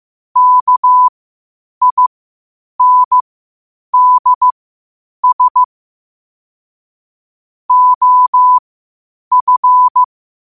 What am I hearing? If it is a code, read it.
Morse code, heard as a keyed tone, about 15 wpm, spaced as for 8 wpm, 1000 Hz. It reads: KINDS OF
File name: qotdfiles / 8wpm